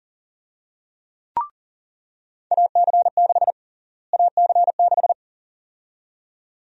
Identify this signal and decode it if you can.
Morse code, heard as a keyed tone, about 40 wpm, 700 Hz. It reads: AC6 AC6
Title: USA Call Sign Prefixes - Repeated 2x - 40wpm